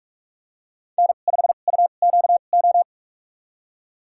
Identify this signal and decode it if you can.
Morse code, heard as a keyed tone, about 46 wpm, spaced as for 32 wpm, 700 Hz. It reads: N5UQO
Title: USA Call Signs - 5 Characters - 2x Character Spacing - 45wpm